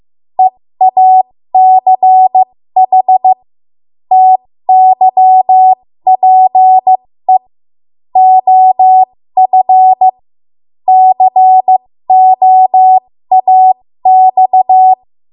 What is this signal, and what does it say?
Morse code, heard as a keyed tone, about 15 wpm, spaced as for 13 wpm, 750 Hz. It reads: EACH TYPE OF COAX
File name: Morse / 13WPM